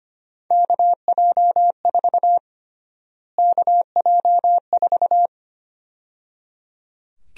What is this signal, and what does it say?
Morse code, heard as a keyed tone, about 25 wpm, 700 Hz. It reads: KJ4 KJ4
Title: USA Call Sign Prefixes - Repeated 2x - 25wpm